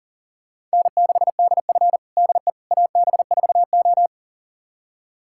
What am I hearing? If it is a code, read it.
Morse code, heard as a keyed tone, about 40 wpm, 700 Hz. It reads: N6DF DE AB4O
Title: USA Callsign DE Callsign - 4 Characters - 40wpm